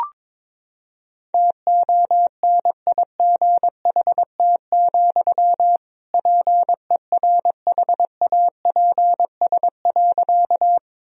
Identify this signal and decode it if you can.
Morse code, heard as a keyed tone, about 22 wpm, 700 Hz. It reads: TONIGHT, PERHAPS.
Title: Sentences from Top 1000 Words - Review Part 3 - 22wpm